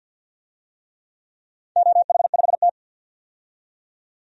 Morse code, heard as a keyed tone, about 52 wpm, 700 Hz. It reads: OH5T
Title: DXCC-100 Call Signs - 4 Characters - 50wpm